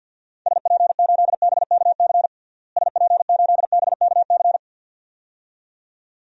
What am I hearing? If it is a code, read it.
Morse code, heard as a keyed tone, about 47 wpm, 700 Hz. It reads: SP8BKC SP8BKC
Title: DXCC-100 Call Signs - 6 Characters - Repeated 2x - 50wpm